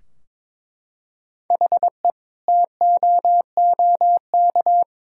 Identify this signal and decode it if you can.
Morse code, heard as a keyed tone, about 21 wpm, 700 Hz. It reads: HE TOOK
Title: Sentences from Top 1000 Words - Review Part 2 - 22wpm